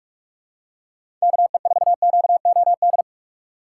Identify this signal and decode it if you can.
Morse code, heard as a keyed tone, about 46 wpm, 700 Hz. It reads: KE4QOD
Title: USA Call Signs - 4x Character Spacing - 45wpm